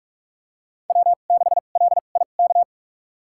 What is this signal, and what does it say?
Morse code, heard as a keyed tone, about 46 wpm, spaced as for 33 wpm, 700 Hz. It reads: W6LIK
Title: USA Call Signs - 5 Characters - 2x Character Spacing - 45wpm